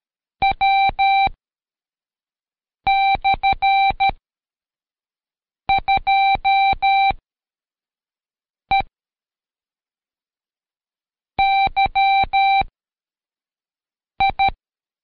Morse code, heard as a keyed tone, about 13 wpm, 780 Hz. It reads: W / 2 E Y I
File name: Code Quick / AudioDownld2 / Seg104-Day12-FiinalExam 1